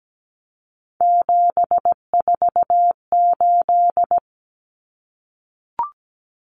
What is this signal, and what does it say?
Morse code, heard as a keyed tone, about 17 wpm, 690 Hz. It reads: 748
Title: Common Cut Numbers - 3-Digit 17wpm